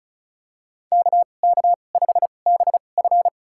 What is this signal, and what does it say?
Morse code, heard as a keyed tone, about 34 wpm, spaced as for 25 wpm, 700 Hz. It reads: KK5BF
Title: USA Call Signs - 5 Characters - 2x Character Spacing - 35wpm